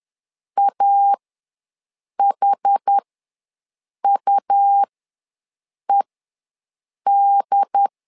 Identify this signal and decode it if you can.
Morse code, heard as a keyed tone, about 10 wpm, spaced as for 6 wpm, 800 Hz. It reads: AHUED
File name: LEZIONI / 2 UHDN / 2 UHD ws 10wpm 2058